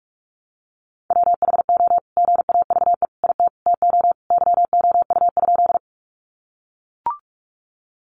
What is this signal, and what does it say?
Morse code, heard as a keyed tone, about 44 wpm, 700 Hz. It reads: WHO GAVE IT TO YOU?
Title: Sentences from Top 500 Words 45wpm